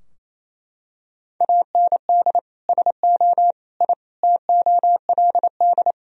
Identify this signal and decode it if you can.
Morse code, heard as a keyed tone, about 28 wpm, 700 Hz. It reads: AND SO I TOLD
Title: Sentences from Top 1000 Words - Review Part 4 - 28wpm